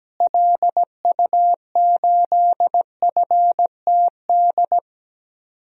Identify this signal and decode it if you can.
Morse code, heard as a keyed tone, about 17 wpm, 700 Hz. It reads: LU8FTD
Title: DXCC-100 Call Signs - 6 Characters - Repeated 2x - 17wpm